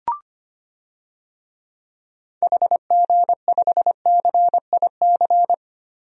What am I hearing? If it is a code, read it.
Morse code, heard as a keyed tone, about 25 wpm, 700 Hz. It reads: HG5CIC
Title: DXCC-100 Call Signs - 6 Characters - 25wpm